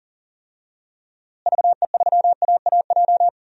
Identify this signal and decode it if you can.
Morse code, heard as a keyed tone, about 37 wpm, 700 Hz. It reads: VE3AAJ
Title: DXCC-100 Call Signs - 6 Characters - Repeated 3x - 40wpm